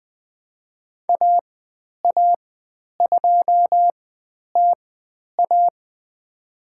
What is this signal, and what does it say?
Morse code, heard as a keyed tone, about 20 wpm, spaced as for 10 wpm, 700 Hz. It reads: AA2TA
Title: 095 Lesson 16g - Letter c - Call Signs 20wpm